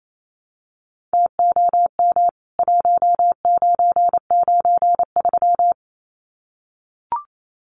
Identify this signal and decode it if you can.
Morse code, heard as a keyed tone, about 28 wpm, 700 Hz. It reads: TOM 1993
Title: CWT Exchange 28wpm